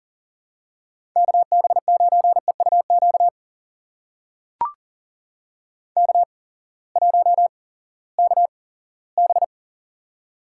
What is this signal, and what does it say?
Morse code, heard as a keyed tone, about 40 wpm, spaced as for 20 wpm, 700 Hz. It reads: KB9EUQ K1KB
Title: USA Call Signs - 8x Character Spacing - 40wpm